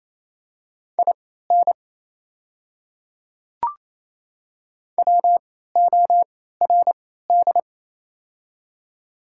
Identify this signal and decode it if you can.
Morse code, heard as a keyed tone, about 28 wpm, spaced as for 16 wpm, 700 Hz. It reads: IN WORD
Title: 3x Character Spacing - Top 100 Words - 28wpm